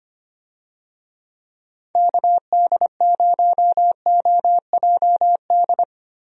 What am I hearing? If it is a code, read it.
Morse code, heard as a keyed tone, about 25 wpm, 700 Hz. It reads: KD0OJD
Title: USA Call Signs - 6 Characters - Repeated 2x - 25wpm